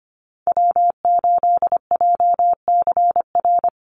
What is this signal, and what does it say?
Morse code, heard as a keyed tone, about 24 wpm, 700 Hz. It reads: W8JCR
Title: USA Call Signs 25wpm